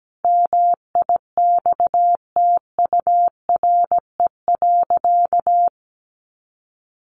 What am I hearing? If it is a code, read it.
Morse code, heard as a keyed tone, about 17 wpm, 700 Hz. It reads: MIXTURE.
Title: Sentences from Top 1500 Words 17wpm